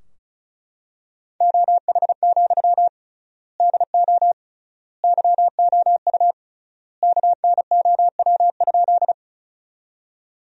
Morse code, heard as a keyed tone, about 36 wpm, 700 Hz. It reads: OH, DO YOU KNOW?
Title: 3x Word-Spacing - Sentences from Top 500 Words - 35wpm